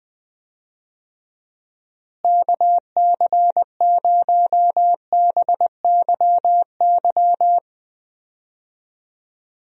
Morse code, heard as a keyed tone, about 20 wpm, 700 Hz. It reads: KC0BYY
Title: USA Call Signs - 6 Characters - Repeated 2x - 20wpm